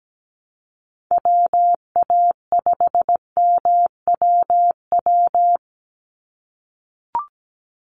Morse code, heard as a keyed tone, about 17 wpm, 700 Hz. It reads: WA5MWW
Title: USA Call Signs 17wpm